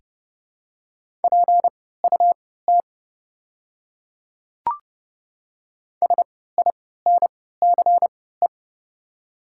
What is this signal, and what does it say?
Morse code, heard as a keyed tone, about 30 wpm, spaced as for 17 wpm, 690 Hz. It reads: PUT SINCE
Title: 3x Character Spacing - Top 300 Words - 30wpm